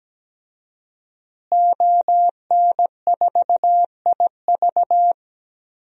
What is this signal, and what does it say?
Morse code, heard as a keyed tone, about 17 wpm, 700 Hz. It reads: ON4IV
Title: DXCC-100 Call Signs - 5 Characters - Repeated 3x - 17wpm